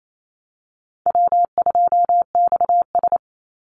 Morse code, heard as a keyed tone, about 28 wpm, 700 Hz. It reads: W2XS
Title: Easy Call Signs Repeated 3x - 28wpm